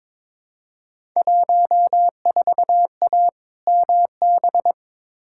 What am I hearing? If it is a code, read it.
Morse code, heard as a keyed tone, about 22 wpm, 700 Hz. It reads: 14A MB
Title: Field Day Exchanges Encore - 22wpm